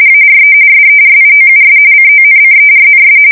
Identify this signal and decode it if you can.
RTTY, about 45.45 baud. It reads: ARE SERIOUS CONCERNS
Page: 170 Shift 2125/2295